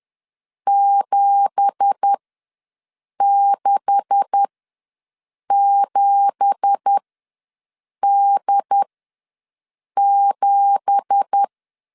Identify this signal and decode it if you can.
Morse code, heard as a keyed tone, about 10 wpm, spaced as for 6 wpm, 800 Hz. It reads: 767D7
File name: LEZIONI / 9 2867 ws 10wpm 4215